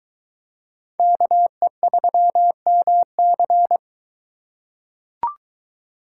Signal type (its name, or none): Morse code